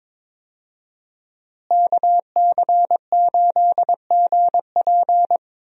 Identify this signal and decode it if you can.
Morse code, heard as a keyed tone, about 22 wpm, 700 Hz. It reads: KC8GP
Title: USA Call Signs - 5 Characters - Repeated 3x - 22wpm